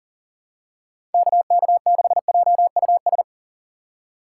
Morse code, heard as a keyed tone, about 40 wpm, 700 Hz. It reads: KK6JUS
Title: USA Call Signs - 6x Character Spacing - 40wpm